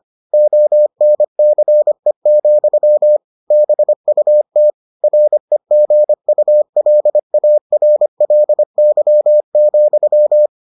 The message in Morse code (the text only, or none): ONCE, BUT REGULARLY,